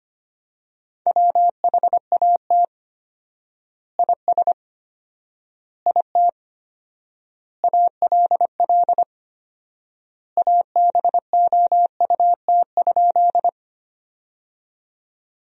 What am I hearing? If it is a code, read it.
Morse code, heard as a keyed tone, about 25 wpm, 700 Hz. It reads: WHAT IS IT ALL ABOUT?
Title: 4x Word-Spacing - Sentences from Top 100 Words - 25wpm